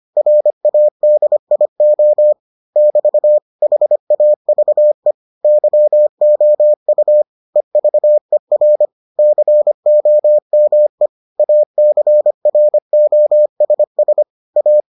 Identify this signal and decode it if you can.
Morse code, heard as a keyed tone, about 25 wpm, 600 Hz. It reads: RADIO <BT> HAVE YOU EVER COME ACROSS A